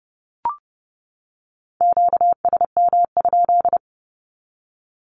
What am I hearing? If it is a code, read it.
Morse code, heard as a keyed tone, about 30 wpm, 700 Hz. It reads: QSM?
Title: Q-Codes 30wpm